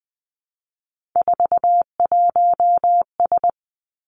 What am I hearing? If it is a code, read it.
Morse code, heard as a keyed tone, about 20 wpm, 700 Hz. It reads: 41S
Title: Three Challenging Letter-Number Combinations 20wpm